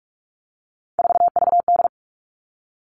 Morse code, heard as a keyed tone, about 44 wpm, 700 Hz. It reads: 4VD
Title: All Cut Numbers - 3-Digit 45wpm